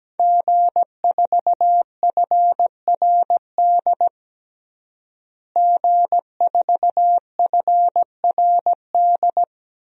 Morse code, heard as a keyed tone, about 17 wpm, 700 Hz. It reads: G4FRD G4FRD
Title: DXCC-100 Call Signs - 5 Characters - Repeated 2x - 17wpm